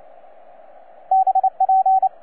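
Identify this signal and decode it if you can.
Morse code, heard as a keyed tone, about 29 wpm, 760 Hz, with radio noise. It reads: BP
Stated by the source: TZ6BB on 12 metres CW, from DX World.net.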